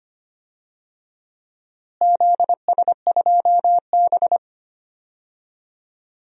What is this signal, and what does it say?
Morse code, heard as a keyed tone, about 25 wpm, 700 Hz. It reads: ZS2B
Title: DXCC-100 Call Signs - 4 Characters - Repeated 2x - 25wpm